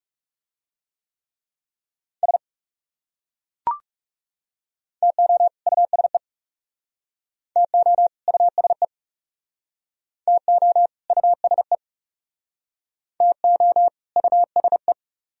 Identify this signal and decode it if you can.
Morse code, heard as a keyed tone, about 38 wpm, 700 Hz. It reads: S TO USE TO USE TO USE TO USE